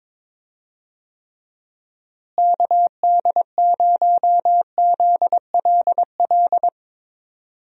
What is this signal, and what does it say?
Morse code, heard as a keyed tone, about 22 wpm, 700 Hz. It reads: KD0ZLL
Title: USA Call Signs - 6 Characters - Repeated 2x - 22wpm